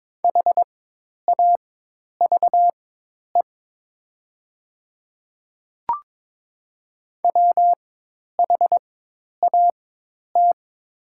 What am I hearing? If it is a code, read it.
Morse code, heard as a keyed tone, about 22 wpm, spaced as for 10 wpm, 700 Hz. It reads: HAVE WHAT
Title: 4x Character Spacing - Top 100 Words - 22wpm